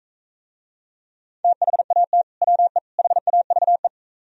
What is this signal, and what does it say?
Morse code, heard as a keyed tone, about 43 wpm, 700 Hz. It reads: THAT WE HAVE